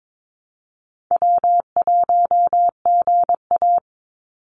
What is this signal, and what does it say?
Morse code, heard as a keyed tone, about 22 wpm, 700 Hz. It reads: W1GA
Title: USA Call Signs 22wpm